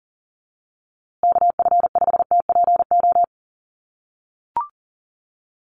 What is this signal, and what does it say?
Morse code, heard as a keyed tone, about 40 wpm, 700 Hz. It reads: KF5TPO
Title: USA Call Signs Repeated 2x 40wpm